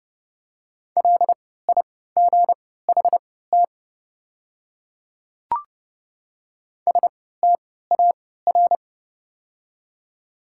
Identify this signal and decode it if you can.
Morse code, heard as a keyed tone, about 32 wpm, spaced as for 17 wpm, 700 Hz. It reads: LIGHT STAR
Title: 3x Character Spacing - Top 500 Words - 30wpm